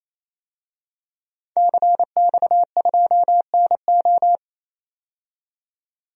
Morse code, heard as a keyed tone, about 28 wpm, 700 Hz. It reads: CX2NO
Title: DXCC-100 Call Signs - 5 Characters - 28wpm